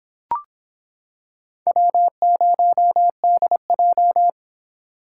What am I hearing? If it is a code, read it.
Morse code, heard as a keyed tone, about 26 wpm, 700 Hz. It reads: W0DJ